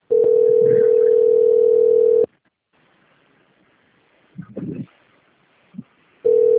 Ringback tone (North America), 440 + 480 Hz.